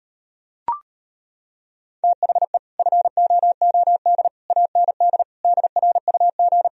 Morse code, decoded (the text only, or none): THE FOOD AND DRUG